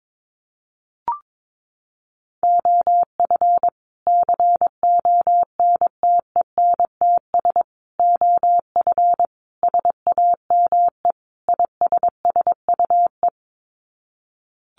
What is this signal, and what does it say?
Morse code, heard as a keyed tone, about 22 wpm, 700 Hz. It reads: OF CONTENTS OF SAME ISSUE
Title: Sets of 5 Words Encore2 22wpm